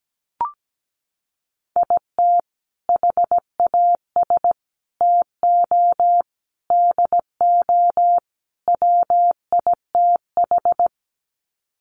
Morse code, heard as a keyed tone, about 17 wpm, 700 Hz. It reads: IT HAS TO DO WITH
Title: Sets of 5 Words 17wpm